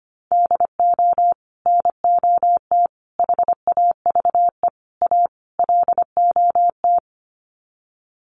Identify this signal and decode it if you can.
Morse code, heard as a keyed tone, about 25 wpm, 700 Hz. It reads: DO NOT HAVE A LOT
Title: Sets of 5 Words Encore 25wpm